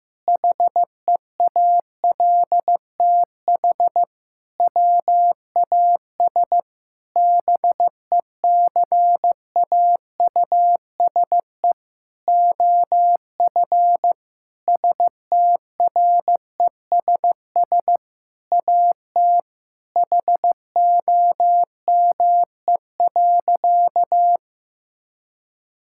Morse code, heard as a keyed tone, about 16 wpm, 700 Hz. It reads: HEALTH WAS BECAUSE OF STRESS AT HOME.
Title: Sentences from Top 1000 Words - Review Part 6 - 15wpm